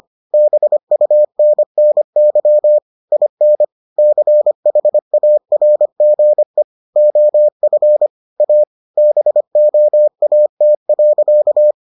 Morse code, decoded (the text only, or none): BUNNY IN CHARGE OF A BOAT.